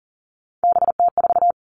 Morse code, heard as a keyed tone, about 40 wpm, 700 Hz. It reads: BT4
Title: All Cut Numbers - 3-Digit 40wpm